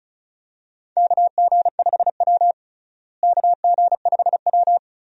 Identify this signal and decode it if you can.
Morse code, heard as a keyed tone, about 36 wpm, 700 Hz. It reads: KG5W KG5W
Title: USA Call Signs - 4 Characters - Repeated 2x - 35wpm